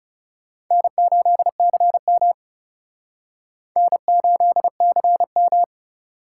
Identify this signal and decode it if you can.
Morse code, heard as a keyed tone, about 38 wpm, 700 Hz. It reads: N8CM N8CM